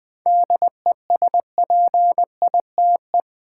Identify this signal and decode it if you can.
Morse code, heard as a keyed tone, about 20 wpm, 700 Hz. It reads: DESPITE
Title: Seven Letter Words - 20wpm